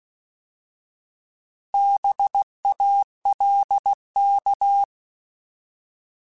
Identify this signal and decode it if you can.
Morse code, heard as a keyed tone, about 16 wpm, 800 Hz. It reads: BALK
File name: Alphabetical / Alphabetical mp3s / Bs @15